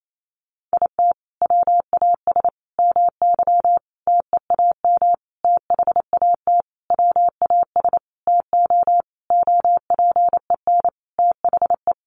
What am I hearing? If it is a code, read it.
Morse code, heard as a keyed tone, about 28 wpm, 700 Hz. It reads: IT WAS MY TEAM THAT WAS TO OPEN THE
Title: Sentences from Top 1500 Words 28wpm